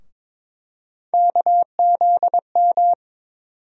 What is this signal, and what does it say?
Morse code, heard as a keyed tone, about 22 wpm, 700 Hz. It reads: KZM
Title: USA Call Sign Postfixes - 3 Letters 22wpm